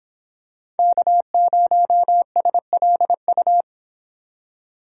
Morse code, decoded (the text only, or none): K0SLU